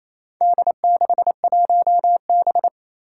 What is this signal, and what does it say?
Morse code, heard as a keyed tone, about 28 wpm, 700 Hz. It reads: D61B
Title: Four Challenging Letter-Number Combinations 28wpm